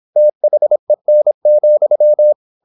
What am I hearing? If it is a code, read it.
Morse code, heard as a keyed tone, about 26 wpm, 600 Hz. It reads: THEN,
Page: Never 26wpm | CW med Gnister